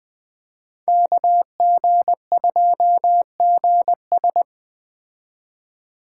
Morse code, heard as a keyed tone, about 21 wpm, 700 Hz. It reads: KG2GS